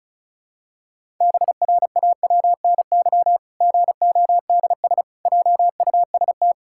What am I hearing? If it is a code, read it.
Morse code, heard as a keyed tone, about 36 wpm, 700 Hz. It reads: BRAWNY GODS JUST
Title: Pangram Challenge v2 - 35wpm